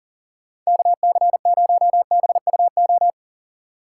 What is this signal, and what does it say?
Morse code, heard as a keyed tone, about 49 wpm, 700 Hz. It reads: KC0BUO
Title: Speed Racing - USA 2x3 Call Signs 40wpm